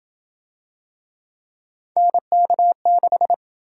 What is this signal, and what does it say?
Morse code, heard as a keyed tone, about 27 wpm, 700 Hz. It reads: NK6